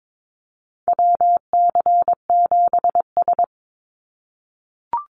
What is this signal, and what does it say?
Morse code, heard as a keyed tone, about 22 wpm, 700 Hz. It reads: WC7S
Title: Call Signs - 22wpm